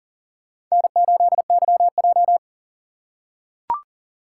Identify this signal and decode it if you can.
Morse code, heard as a keyed tone, about 40 wpm, 700 Hz. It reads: N8YJ
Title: USA Call Signs - 4 Characters - 8x Character Spacing - 40wpm